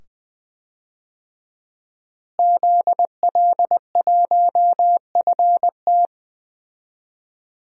Morse code, heard as a keyed tone, about 20 wpm, 700 Hz. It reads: ZL1FT
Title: DXCC-100 Call Signs - 5 Characters - Repeated 2x - 20wpm